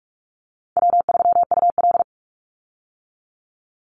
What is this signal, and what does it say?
Morse code, heard as a keyed tone, about 44 wpm, 700 Hz. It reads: W3UL
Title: Easy Call Signs Repeated 2x - 45wpm